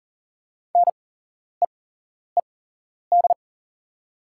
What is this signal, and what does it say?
Morse code, heard as a keyed tone, about 40 wpm, spaced as for 11 wpm, 700 Hz. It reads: NEED
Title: 8x Character Spacing - Top 200 Words - 40wpm